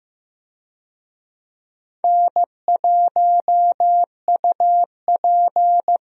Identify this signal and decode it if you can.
Morse code, heard as a keyed tone, about 15 wpm, 700 Hz. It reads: N1UP
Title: USA Call Signs - 4 Characters - 15wpm